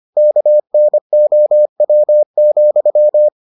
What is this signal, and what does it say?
Morse code, heard as a keyed tone, about 25 wpm, 600 Hz. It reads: KNOW,